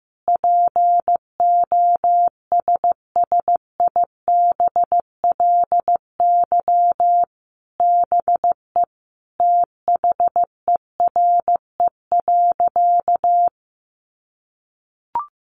Morse code, heard as a keyed tone, about 15 wpm, 700 Hz. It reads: POSSIBLY BE THERE.
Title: Sentences from Top 1300 Words 15wpm